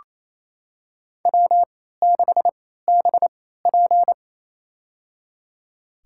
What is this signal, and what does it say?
Morse code, heard as a keyed tone, about 28 wpm, spaced as for 16 wpm, 700 Hz. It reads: W6BP
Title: USA Call Signs - 4 Characters - 3x Character Spacing - 28wpm